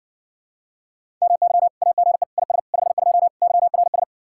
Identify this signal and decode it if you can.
Morse code, heard as a keyed tone, about 57 wpm, 700 Hz. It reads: MY AGE IS 52 YRS